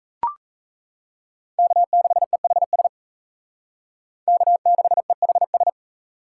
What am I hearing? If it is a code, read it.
Morse code, heard as a keyed tone, about 40 wpm, 700 Hz. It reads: K6EHS K6EHS